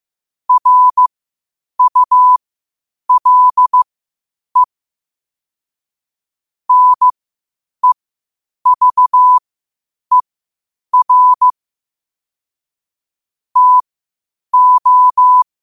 Morse code, heard as a keyed tone, about 15 wpm, spaced as for 8 wpm, 1000 Hz. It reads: RULE NEVER TO
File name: qotdfiles / 8wpm